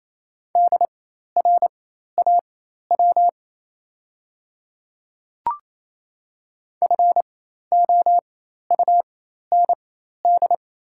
Morse code, heard as a keyed tone, about 27 wpm, spaced as for 13 wpm, 690 Hz. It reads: DRAW FOUND
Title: 4x Character Spacing - Top 300 Words - 28wpm